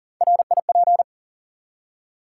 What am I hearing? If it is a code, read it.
Morse code, heard as a keyed tone, about 40 wpm, 700 Hz. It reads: RIP
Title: Three Letter Words - 40wpm